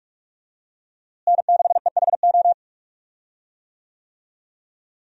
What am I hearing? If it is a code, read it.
Morse code, heard as a keyed tone, about 42 wpm, 700 Hz. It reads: N6EHO
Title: USA Call Signs - 5 Characters - Repeated 2x - 45wpm